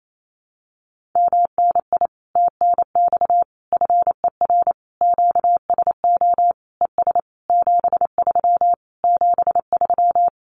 Morse code, heard as a keyed tone, about 28 wpm, 700 Hz. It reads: MNI TNX FER QSO ES 73 73
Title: Common QSO Phrases Encore 28wpm